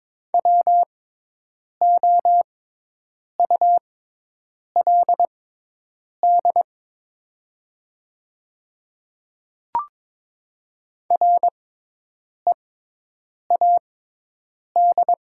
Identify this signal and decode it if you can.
Morse code, heard as a keyed tone, about 22 wpm, spaced as for 8 wpm, 700 Hz. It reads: WOULD READ
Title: 6x Character Spacing - Top 200 Words - 22wpm